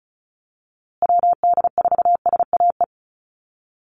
Morse code, heard as a keyed tone, about 34 wpm, 700 Hz. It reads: WD4SAE
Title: USA Call Signs Repeated 3x 35wpm